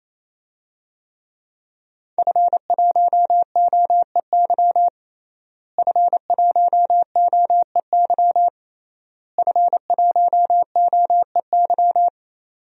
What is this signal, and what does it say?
Morse code, heard as a keyed tone, about 28 wpm, 700 Hz. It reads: F1OEY F1OEY F1OEY
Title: DXCC-100 Call Signs - 5 Characters - Repeated 3x - 28wpm